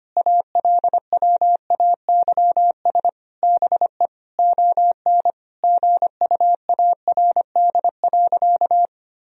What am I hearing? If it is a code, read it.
Morse code, heard as a keyed tone, about 25 wpm, 700 Hz. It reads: ALWAYS BE ON GUARD.
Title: Sentences from Top 1000 Words - Review Part 4 - 25wpm